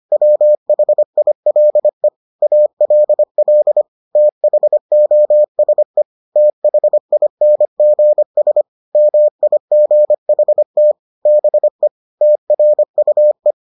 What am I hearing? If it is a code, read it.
Morse code, heard as a keyed tone, about 25 wpm, 600 Hz. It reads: WHILE ALL THOSE THINGS MIGHT BE TRUE